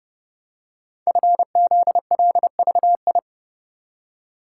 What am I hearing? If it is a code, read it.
Morse code, heard as a keyed tone, about 30 wpm, 700 Hz. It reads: FZLVI
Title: Code Groups - 5 Letters - 30wpm